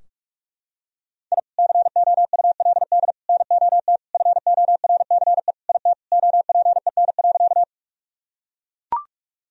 Morse code, heard as a keyed tone, about 41 wpm, 700 Hz. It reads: I COULD NOT FORCE IT OPEN.
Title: Sentences from Top 1000 Words - Review Part 3 - 45wpm